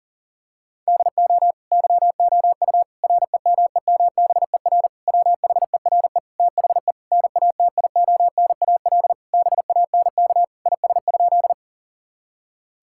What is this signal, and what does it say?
Morse code, heard as a keyed tone, about 40 wpm, 700 Hz. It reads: DO YOU REMEMBER WHERE THE NATIONAL BANK IS?
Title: Sentences from Top 1000 Words - Review Part 6 - 40wpm